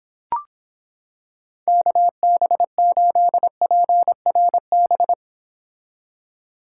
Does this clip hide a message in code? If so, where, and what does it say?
Yes: Morse code from 1.7 to 5.1 s, heard as a keyed tone, about 26 wpm, 700 Hz. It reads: KB8PRB